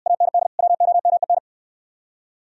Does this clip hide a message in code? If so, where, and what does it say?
Yes: Morse code from 0.1 to 1.4 s, heard as a keyed tone, about 65 wpm, 700 Hz. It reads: AND WOMEN